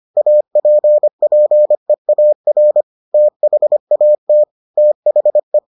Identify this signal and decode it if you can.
Morse code, heard as a keyed tone, about 25 wpm, 600 Hz. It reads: APPEAR THAT THE